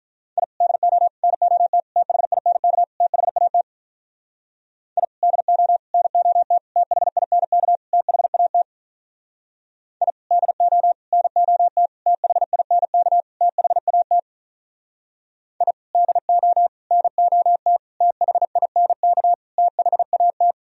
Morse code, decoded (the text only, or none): I DO NOT THINK THAT I DO NOT THINK THAT I DO NOT THINK THAT I DO NOT THINK THAT